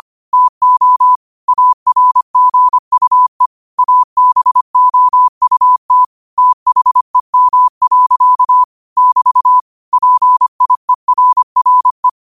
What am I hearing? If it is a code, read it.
Morse code, heard as a keyed tone, about 25 wpm, 1000 Hz. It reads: TO ARGUE ABOUT THEM. <BT> PIERRE
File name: qotdfiles / 25wpm